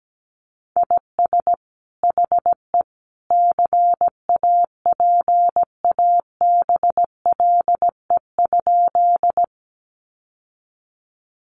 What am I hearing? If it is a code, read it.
Morse code, heard as a keyed tone, about 17 wpm, 700 Hz. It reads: IS HE CAPABLE?
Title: Sentences from Top 1200 Words 17wpm